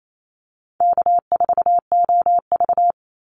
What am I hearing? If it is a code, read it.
Morse code, heard as a keyed tone, about 28 wpm, 700 Hz. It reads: K4OV
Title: Call Signs Repeated 2x - 28wpm